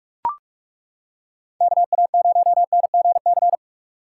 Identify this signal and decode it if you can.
Morse code, heard as a keyed tone, about 36 wpm, 700 Hz. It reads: KA0NGC
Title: Speed Racing - USA Call Signs 30wpm